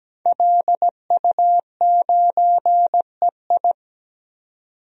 Morse code, heard as a keyed tone, about 17 wpm, 700 Hz. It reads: LU9EI
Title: DXCC-100 Call Signs - 5 Characters - 17wpm